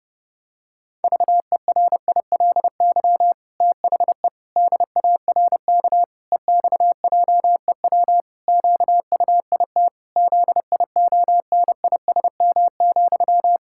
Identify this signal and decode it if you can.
Morse code, heard as a keyed tone, about 30 wpm, 700 Hz. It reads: VERILY THE DARK EXJEW QUIT ZIONISM,
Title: Pangram Challenge v2 - 30wpm